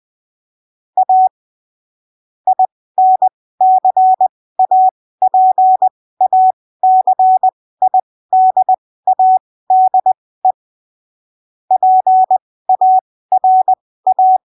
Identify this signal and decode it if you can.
Morse code, heard as a keyed tone, about 20 wpm, spaced as for 15 wpm, 750 Hz. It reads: A INCAPACIDADE PARA
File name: arquivos / texto / texto 24 20ppm